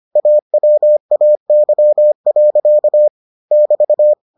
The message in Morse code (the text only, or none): AWAY. <BT>